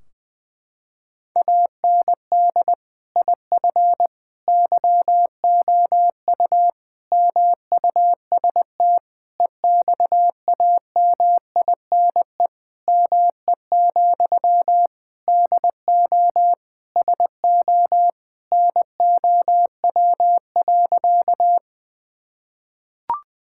Morse code, heard as a keyed tone, about 20 wpm, 700 Hz. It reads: AND IF YOU MUST EXAMINE ME, DO SO NOW.
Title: Sentences from Top 1600 Words 20wpm